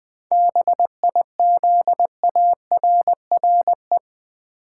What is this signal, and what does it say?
Morse code, heard as a keyed tone, about 20 wpm, 700 Hz. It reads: BIZARRE
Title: 7 Letter Words - First 3 Letters Given - 20wpm